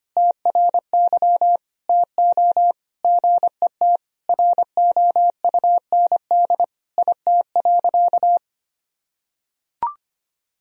Morse code, heard as a keyed tone, about 25 wpm, 700 Hz. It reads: TRY TO GET ROUND IT.
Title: Sentences from Top 1000 Words - Review Part 1 - 25wpm